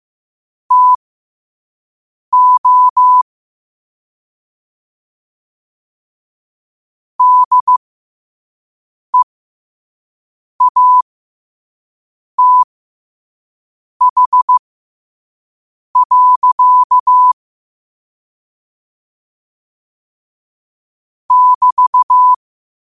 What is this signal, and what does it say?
Morse code, heard as a keyed tone, about 15 wpm, spaced as for 5 wpm, 1000 Hz. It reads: TO DEATH. <BT>
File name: qotdfiles / 5wpm